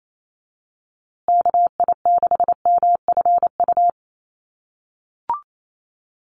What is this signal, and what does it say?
Morse code, heard as a keyed tone, about 28 wpm, 700 Hz. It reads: KI6MFU
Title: USA Call Signs Repeated 3x 28wpm